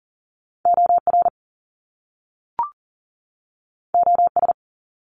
Morse code, heard as a keyed tone, about 40 wpm, 700 Hz. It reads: OR OS
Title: Top Two Letter Combinations in Words 40wpm